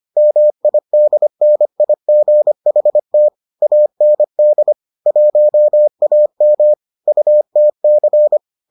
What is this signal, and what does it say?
Morse code, heard as a keyed tone, about 25 wpm, 600 Hz. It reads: MIDNIGHT AND 1AM UTC